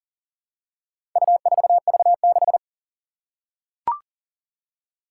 Morse code, heard as a keyed tone, about 40 wpm, 700 Hz. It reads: U4V6
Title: Four Challenging Letter-Number Combinations 40wpm